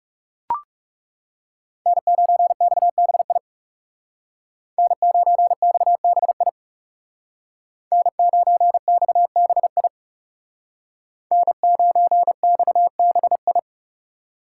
Morse code, heard as a keyed tone, about 38 wpm, 700 Hz. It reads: N9XBI N9XBI N9XBI N9XBI